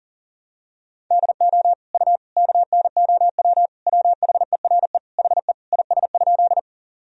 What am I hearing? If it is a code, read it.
Morse code, heard as a keyed tone, about 40 wpm, 700 Hz. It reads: DO U KNOW WHERE HE IS?
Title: Sentences from Top 300 Words and Abbreviations 40wpm